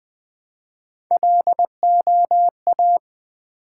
Morse code, heard as a keyed tone, about 20 wpm, 700 Hz. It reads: LOA
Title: Build A Word - 1 Letter At A Time - 4-Letter Words - 20wpm